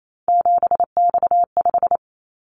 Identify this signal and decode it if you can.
Morse code, heard as a keyed tone, about 28 wpm, 700 Hz. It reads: 7X5
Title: Callsign Prefixes for Top 100 DXCC - 28wpm